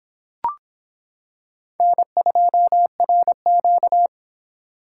Morse code, heard as a keyed tone, about 25 wpm, 700 Hz. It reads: N2RQ